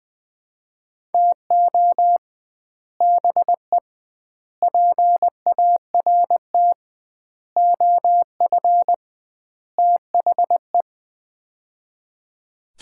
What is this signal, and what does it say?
Morse code, heard as a keyed tone, about 20 wpm, 700 Hz. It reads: TO BE PART OF THE
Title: 2x Word-Spacing - Sets of 5 Words - 20wpm